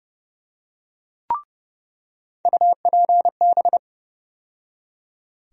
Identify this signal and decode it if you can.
Morse code, heard as a keyed tone, about 30 wpm, 700 Hz. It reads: UPB
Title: USA Call Sign Postfixes - 3 Letters 30wpm